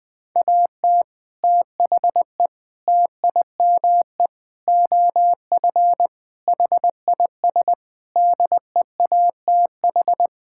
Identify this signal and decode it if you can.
Morse code, heard as a keyed tone, about 20 wpm, 700 Hz. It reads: AT THE TIME OF HIS DEATH
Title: Sets of 6 Words - 20wpm